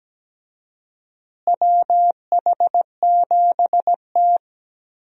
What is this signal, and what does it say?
Morse code, heard as a keyed tone, about 17 wpm, 700 Hz. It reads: WH7T
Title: USA Call Signs - 4 Characters - Repeated 2x - 17wpm